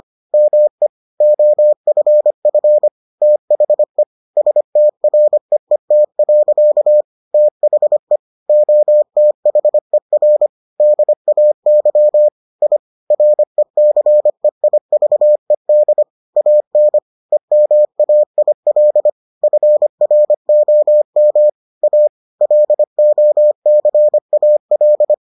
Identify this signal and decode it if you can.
Morse code, heard as a keyed tone, about 25 wpm, 600 Hz. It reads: ME OFF THE STREET. THE OTHER DAY I RECEIVED AN EMAIL FROM A LOCAL